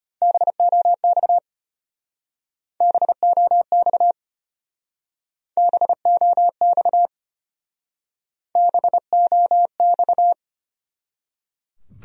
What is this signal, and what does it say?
Morse code, heard as a keyed tone, about 30 wpm, 700 Hz. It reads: BOX BOX BOX BOX
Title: Speed-Racing Top 500 Words 25wpm